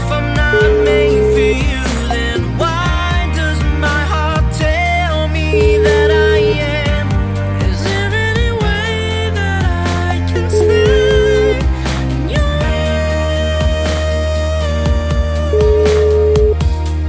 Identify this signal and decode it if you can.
Ringback tone (Europe, 425 Hz), with noise.